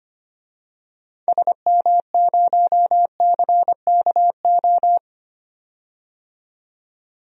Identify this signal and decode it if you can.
Morse code, heard as a keyed tone, about 25 wpm, 700 Hz. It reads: SM0CKO
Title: DXCC-100 Call Signs - 6 Characters - 25wpm